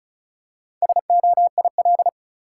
Morse code, heard as a keyed tone, about 36 wpm, 700 Hz. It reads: SOIL
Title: Four Letter Words - 35wpm